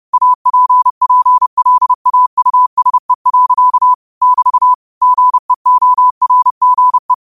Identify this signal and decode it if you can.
Morse code, heard as a keyed tone, about 30 wpm, 1000 Hz. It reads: APPLAUSE. <BT> GEORGE
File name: qotdfiles / 30wpm